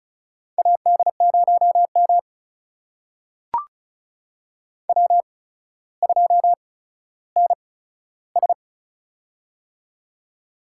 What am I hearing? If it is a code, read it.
Morse code, heard as a keyed tone, about 36 wpm, spaced as for 15 wpm, 700 Hz. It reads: AD0M W2NS
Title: USA Call Signs - 8x Character Spacing - 35wpm